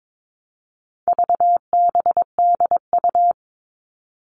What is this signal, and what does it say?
Morse code, heard as a keyed tone, about 22 wpm, 700 Hz. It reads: VBDU
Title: All Cut Numbers - 4-Digit 22wpm